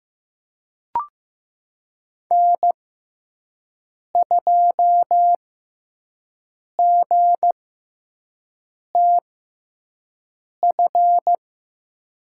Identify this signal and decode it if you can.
Morse code, heard as a keyed tone, about 15 wpm, spaced as for 5 wpm, 700 Hz. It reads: N2GTF
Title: USA Call Signs - 6x Character Spacing - 15wpm